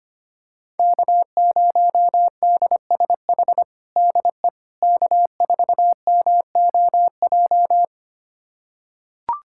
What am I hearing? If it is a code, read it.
Morse code, heard as a keyed tone, about 25 wpm, 700 Hz. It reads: K0DSH DE K4MOJ
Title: USA Callsign DE Callsign - 5 Characters - 25wpm